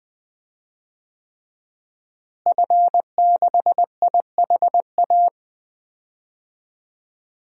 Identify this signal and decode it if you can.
Morse code, heard as a keyed tone, about 20 wpm, 700 Hz. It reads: F6IHA
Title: DXCC-100 Call Signs - 5 Characters - Repeated 3x - 20wpm